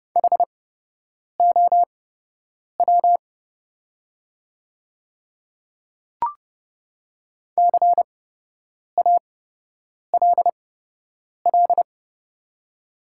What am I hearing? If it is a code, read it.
Morse code, heard as a keyed tone, about 30 wpm, spaced as for 8 wpm, 700 Hz. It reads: HOW CALL
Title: 8x Character Spacing - Top 200 Words - 30wpm